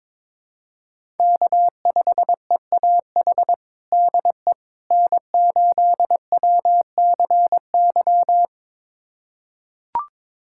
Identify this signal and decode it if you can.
Morse code, heard as a keyed tone, about 22 wpm, 700 Hz. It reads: K5EAH DE N8WCY
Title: USA Callsign DE Callsign - 5 Characters - 22wpm